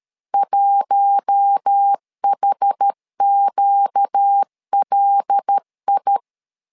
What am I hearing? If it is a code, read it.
Morse code, heard as a keyed tone, about 13 wpm, 800 Hz. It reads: 1HQLI
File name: LEZIONI / 5 LBVC / 5 LB 12wpm 1906